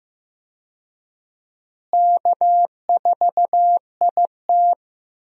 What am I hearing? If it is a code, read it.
Morse code, heard as a keyed tone, about 15 wpm, 700 Hz. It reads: K4IT
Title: USA Call Signs - 4 Characters - Repeated 3x - 15wpm